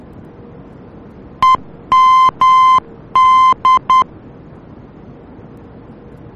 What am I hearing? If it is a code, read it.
Morse code, heard as a keyed tone, about 10 wpm, 1040 Hz. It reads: EMD
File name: germany / mx / EMD-D